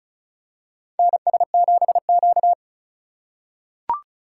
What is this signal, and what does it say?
Morse code, heard as a keyed tone, about 34 wpm, 700 Hz. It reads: NS7Q
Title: USA Call Signs - 4 Characters - 4x Character Spacing - 35wpm